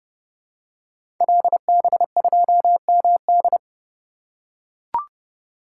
Morse code, heard as a keyed tone, about 30 wpm, 700 Hz. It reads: LB2MD
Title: Code Groups - 5 Letter:Numbers - 30wpm